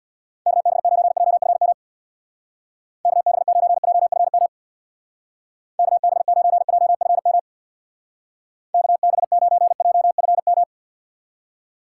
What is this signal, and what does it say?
Morse code, heard as a keyed tone, about 62 wpm, 700 Hz. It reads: KB9JFM KB9JFM KB9JFM KB9JFM
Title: Speed Racing - USA Call Signs 50wpm